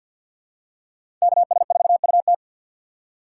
Morse code, heard as a keyed tone, about 47 wpm, 690 Hz. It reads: KS4UT
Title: USA Call Signs - 5 Characters - 6x Character Spacing - 50wpm